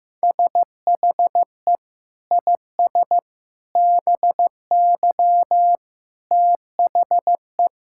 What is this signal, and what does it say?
Morse code, heard as a keyed tone, about 15 wpm, 700 Hz. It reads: SHE IS BY THE
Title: Sentences from Top 1600 Words 15wpm